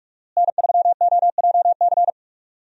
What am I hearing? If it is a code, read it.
Morse code, heard as a keyed tone, about 46 wpm, 700 Hz. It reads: N3OJC
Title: USA Call Signs - 5 Characters - Repeated 2x - 45wpm